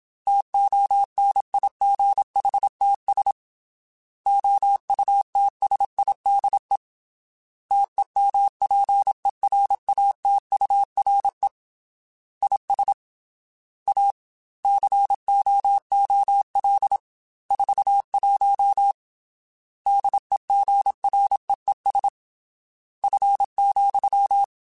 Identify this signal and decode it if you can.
Morse code, heard as a keyed tone, about 27 wpm, 800 Hz. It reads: TONIGHTS OUTSIDE TEMPERATURE IS A COOL 41 DEGREES F,
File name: Weather Sets / Weather Sets-mp3s / Weather 4 14 @25